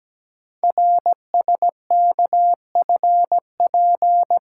Morse code, heard as a keyed tone, about 17 wpm, 700 Hz. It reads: RSKFP
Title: Code Groups - 5 Letters - 17wpm